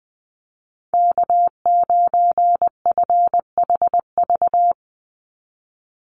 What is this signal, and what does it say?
Morse code, heard as a keyed tone, about 20 wpm, 700 Hz. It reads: K9FHV
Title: USA Call Signs 20wpm